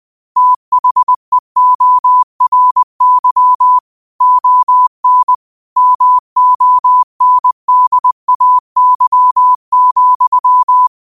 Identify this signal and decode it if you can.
Morse code, heard as a keyed tone, about 20 wpm, 1000 Hz. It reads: THEORY ON MONDAY,
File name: qotdfiles / 20wpm